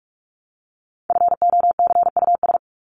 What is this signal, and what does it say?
Morse code, heard as a keyed tone, about 44 wpm, 700 Hz. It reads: FOCUS
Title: Top 501-750 Words 45wpm